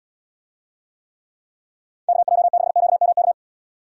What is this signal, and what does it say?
Morse code, heard as a keyed tone, about 78 wpm, 700 Hz. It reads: KJ6QMK